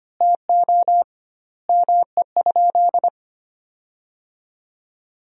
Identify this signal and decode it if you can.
Morse code, heard as a keyed tone, about 25 wpm, 700 Hz. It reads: TO ME?
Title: 2x Word-Spacing - Sentences from Top 200 Words - 25wpm